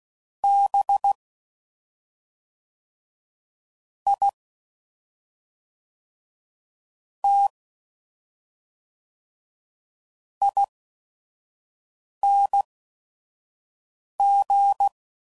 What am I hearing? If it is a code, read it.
Morse code, heard as a keyed tone, about 16 wpm, spaced as for 5 wpm, 800 Hz. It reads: B I T ING
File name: Suffixes / Special ING's / SlowToFaster ING's